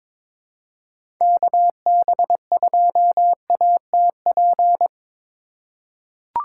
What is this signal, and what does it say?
Morse code, heard as a keyed tone, about 22 wpm, 700 Hz. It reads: KB2ATP
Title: USA Call Signs - 4x Character Spacing - 22wpm